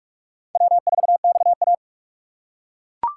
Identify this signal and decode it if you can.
Morse code, heard as a keyed tone, about 42 wpm, 700 Hz. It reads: W4XA